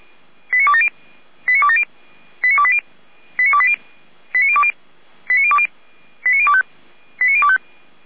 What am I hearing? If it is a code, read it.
Five-tone selective call (ZVEI), heard as quick runs of five tones, tones of about 70 ms. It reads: 89270 89270 89280 89280 89020 89020 89025 89025